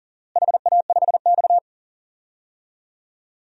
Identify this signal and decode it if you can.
Morse code, heard as a keyed tone, about 40 wpm, 700 Hz. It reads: HA5X
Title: DXCC-100 Call Signs - 4 Characters - Repeated 2x - 40wpm